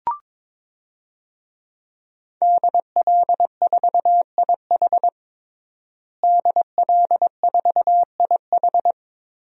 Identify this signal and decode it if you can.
Morse code, heard as a keyed tone, about 21 wpm, 700 Hz. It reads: DL4IH DL4IH
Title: DXCC-100 Call Signs - 5 Characters - Repeated 3x - 22wpm